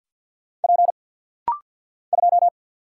Morse code, heard as a keyed tone, about 48 wpm, 700 Hz. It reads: P 2
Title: Single Letter-Number - Rapid-Fire 50wpm